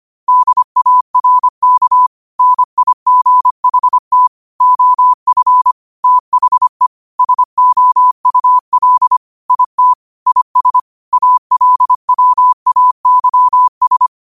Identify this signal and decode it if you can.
Morse code, heard as a keyed tone, about 25 wpm, 1000 Hz. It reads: DARK NIGHT OF THE SOUL IT IS ALWAYS
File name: 25wpm